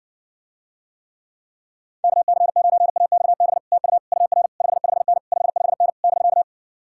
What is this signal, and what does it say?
Morse code, heard as a keyed tone, about 67 wpm, 700 Hz. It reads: KC9AXB TU UR 55N 55N <BK>